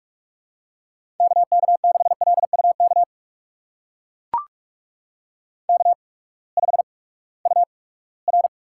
Morse code, heard as a keyed tone, about 44 wpm, spaced as for 13 wpm, 700 Hz. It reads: KK6LU KK5UR
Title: USA Call Signs - 8x Character Spacing - 45wpm